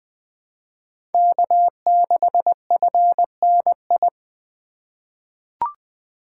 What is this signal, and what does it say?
Morse code, heard as a keyed tone, about 21 wpm, 700 Hz. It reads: K6FNI
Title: USA Call Signs - 3x Character Spacing - 20wpm